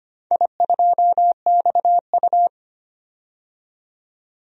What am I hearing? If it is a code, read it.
Morse code, heard as a keyed tone, about 25 wpm, 700 Hz. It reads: I2XU
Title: DXCC-100 Call Signs - 4 Characters - 25wpm